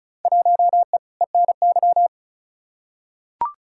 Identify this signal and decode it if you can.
Morse code, heard as a keyed tone, about 34 wpm, 700 Hz. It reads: 1E ENY
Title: Field Day Exchanges - 35wpm